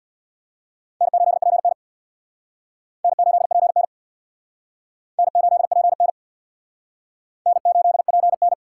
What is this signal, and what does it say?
Morse code, heard as a keyed tone, about 59 wpm, 700 Hz. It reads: N8PN N8PN N8PN N8PN
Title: Speed Racing - USA 1x2 Call Signs 50wpm